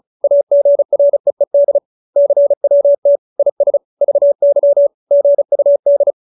Morse code, heard as a keyed tone, about 34 wpm, 570 Hz. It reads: AGREED CWT IS VY GUD